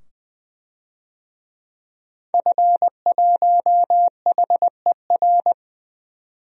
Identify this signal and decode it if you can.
Morse code, heard as a keyed tone, about 20 wpm, 700 Hz. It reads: F1HER
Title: DXCC-100 Call Signs - 5 Characters - 20wpm